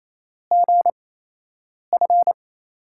Morse code, heard as a keyed tone, about 28 wpm, spaced as for 8 wpm, 700 Hz. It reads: GF
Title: USA Call Signs - 5 Characters - 8x Character Spacing - 28wpm